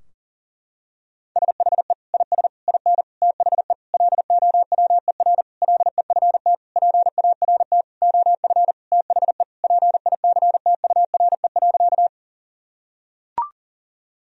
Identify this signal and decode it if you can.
Morse code, heard as a keyed tone, about 37 wpm, 700 Hz. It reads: SHE IS IN THE LOWER LEFT PART OF THE PICTURE.
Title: Sentences from Top 1000 Words - Review Part 4 - 40wpm